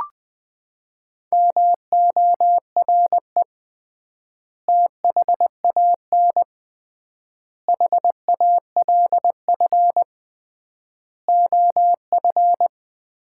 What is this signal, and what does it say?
Morse code, heard as a keyed tone, about 20 wpm, 700 Hz. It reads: MORE THAN HALF OF
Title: 3x Word-Spacing - Sets of 5 Words - 20wpm